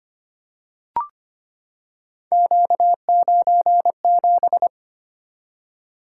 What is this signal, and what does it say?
Morse code, heard as a keyed tone, about 25 wpm, 700 Hz. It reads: Q97
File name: Three Letter-Number Combinations - 25wpm